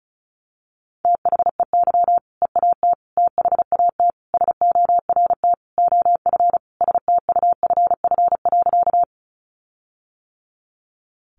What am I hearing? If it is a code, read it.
Morse code, heard as a keyed tone, about 34 wpm, 700 Hz. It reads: THEY EAT THAT SORT OF STUFF.
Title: Sentences from Top 1400 Words 35wpm